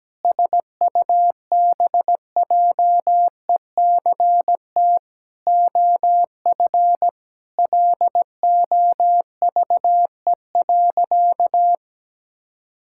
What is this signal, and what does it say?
Morse code, heard as a keyed tone, about 17 wpm, 700 Hz. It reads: SUBJECT OF LOVE.
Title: Sentences from Top 1600 Words 17wpm